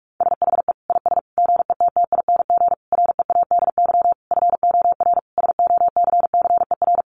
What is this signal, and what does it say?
Morse code, heard as a keyed tone, about 44 wpm, 700 Hz. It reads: SHE IS GETTING READY FOR SOCCER
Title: Easy Phrases 45wpm